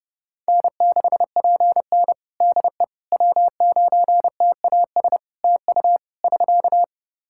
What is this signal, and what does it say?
Morse code, heard as a keyed tone, about 29 wpm, 700 Hz. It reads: N6PN DE W9TAS TU <SK>
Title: Practice QSO - Exchange 4 - Ending - 30wpm